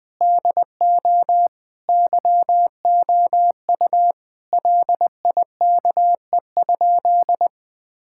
Morse code, heard as a keyed tone, about 21 wpm, 700 Hz. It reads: DO YOU LIKE?
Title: Sentences with 3-Word Phrases - 20wpm